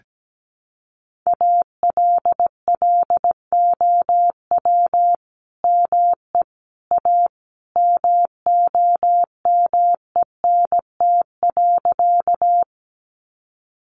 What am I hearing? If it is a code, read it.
Morse code, heard as a keyed tone, about 17 wpm, 700 Hz. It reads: ALLOW ME A MOMENT.
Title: Sentences from Top 1000 Words 17wpm